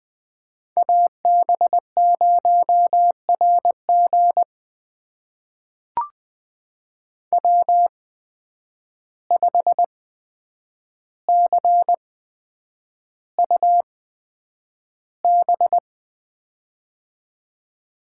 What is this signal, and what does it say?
Morse code, heard as a keyed tone, about 20 wpm, spaced as for 9 wpm, 700 Hz. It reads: AB0RG W5CUB
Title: USA Call Signs - 5 Characters - 8x Character Spacing - 20wpm